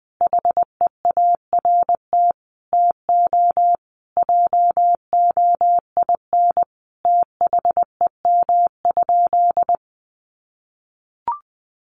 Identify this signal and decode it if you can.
Morse code, heard as a keyed tone, about 19 wpm, 700 Hz. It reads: HEART TO JOIN THEM?
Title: Sentences from Top 1400 Words 20wpm